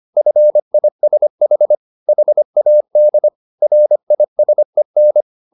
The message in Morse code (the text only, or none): FISH HAD RISEN